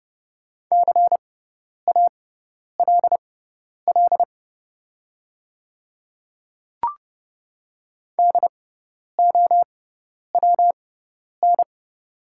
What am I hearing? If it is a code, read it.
Morse code, heard as a keyed tone, about 30 wpm, spaced as for 10 wpm, 700 Hz. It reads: CALL DOWN
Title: 6x Character Spacing - Top 100 Words - 30wpm